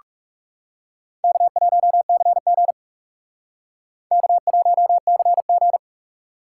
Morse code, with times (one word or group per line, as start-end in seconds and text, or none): 1.2-2.7 K1CG
4.1-5.8 K1CG